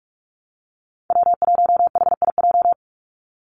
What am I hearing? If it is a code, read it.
Morse code, heard as a keyed tone, about 44 wpm, 700 Hz. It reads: W1HIJ
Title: Call Signs Repeated 2x - 45wpm